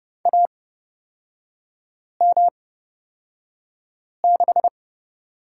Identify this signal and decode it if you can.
Morse code, heard as a keyed tone, about 30 wpm, spaced as for 5 wpm, 700 Hz. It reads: AM6
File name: Lesson 36e - BT - Three Characters 30wpm